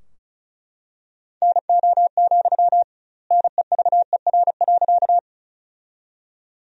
Morse code, heard as a keyed tone, about 34 wpm, 700 Hz. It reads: NO, NEVER.
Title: 2x Word-Spacing - Sentences from Top 300 Words - 35wpm